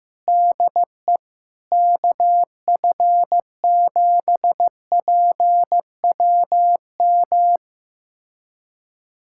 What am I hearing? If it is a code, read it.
Morse code, heard as a keyed tone, about 15 wpm, 700 Hz. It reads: DE KF7PWM
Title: USA Callsign DE Callsign 15wpm